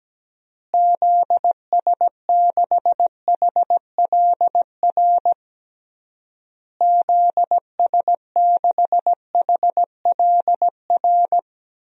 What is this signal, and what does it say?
Morse code, heard as a keyed tone, about 17 wpm, 700 Hz. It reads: ZS6HLR ZS6HLR
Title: DXCC-100 Call Signs - 6 Characters - Repeated 3x - 17wpm